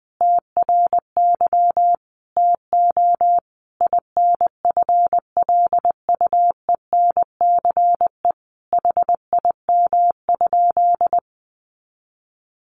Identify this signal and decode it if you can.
Morse code, heard as a keyed tone, about 19 wpm, 700 Hz. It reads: TRY TO INFLUENCE HIM?
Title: Sentences from Top 1100 Words 20wpm